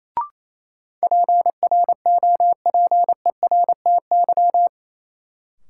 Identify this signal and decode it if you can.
Morse code, heard as a keyed tone, about 28 wpm, 700 Hz. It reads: PROPERTY
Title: Sentences from Top 1000 Words - New Word Drill - 28wpm